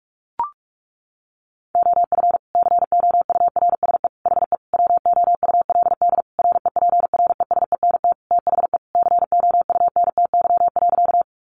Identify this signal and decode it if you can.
Morse code, heard as a keyed tone, about 44 wpm, 700 Hz. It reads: OF COURSE HE WOULD REPRESENT THE COUNTY.
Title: Sentences from Top 1200 Words 45wpm